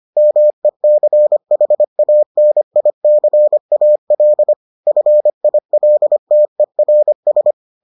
Morse code, heard as a keyed tone, about 25 wpm, 600 Hz. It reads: MECHANICAL FILTERS